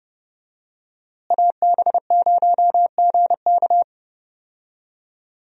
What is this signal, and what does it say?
Morse code, heard as a keyed tone, about 30 wpm, 700 Hz. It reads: AB0GK
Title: USA Call Signs - 5 Characters - Repeated 2x - 30wpm